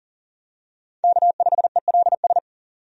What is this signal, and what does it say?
Morse code, heard as a keyed tone, about 40 wpm, 700 Hz. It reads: K5ELS
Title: USA Call Signs - 5 Characters - 40wpm